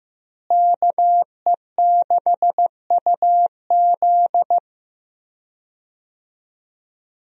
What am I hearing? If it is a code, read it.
Morse code, heard as a keyed tone, about 15 wpm, 700 Hz. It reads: KE6UZ
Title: USA Call Signs - 5 Characters - Repeated 3x - 15wpm